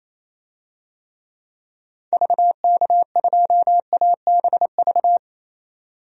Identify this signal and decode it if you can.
Morse code, heard as a keyed tone, about 27 wpm, 700 Hz. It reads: VK2ABV
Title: DXCC-100 Call Signs - 6 Characters - 28wpm